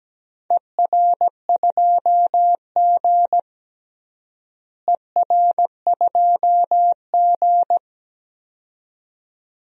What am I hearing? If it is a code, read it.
Morse code, heard as a keyed tone, about 17 wpm, 700 Hz. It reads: ER2G ER2G
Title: DXCC-100 Call Signs - 4 Characters - Repeated 2x - 17wpm